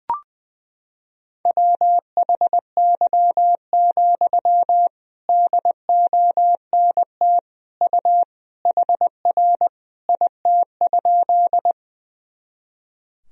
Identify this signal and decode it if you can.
Morse code, heard as a keyed tone, about 20 wpm, 700 Hz. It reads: WHY, DONT U HR IT?
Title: Sentences from Top 300 Words and Abbreviations 20wpm